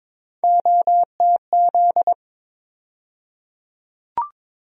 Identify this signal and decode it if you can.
Morse code, heard as a keyed tone, about 22 wpm, 700 Hz. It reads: OTZ
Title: USA Call Sign Postfixes - 3 Letters 22wpm